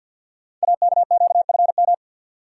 Morse code, heard as a keyed tone, about 46 wpm, 690 Hz. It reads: AKQFM
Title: Code Groups - 5 Letter:Numbers - 50wpm